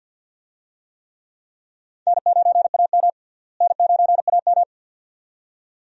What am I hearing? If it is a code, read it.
Morse code, heard as a keyed tone, about 49 wpm, 700 Hz. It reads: N9AM N9AM
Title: USA Call Signs - 4 Characters - Repeated 2x - 50wpm